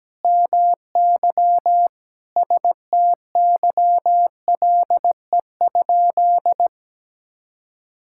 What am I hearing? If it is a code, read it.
Morse code, heard as a keyed tone, about 17 wpm, 700 Hz. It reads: MY STYLE?
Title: Sentences from Top 1000 Words - Review Part 5 - 17wpm